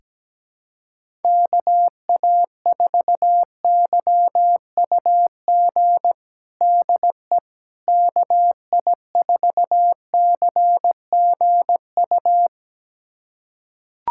Morse code, heard as a keyed tone, about 17 wpm, 700 Hz. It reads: KA4YUG DE KI4CGU
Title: USA Callsign DE Callsign - 6 Characters - 17wpm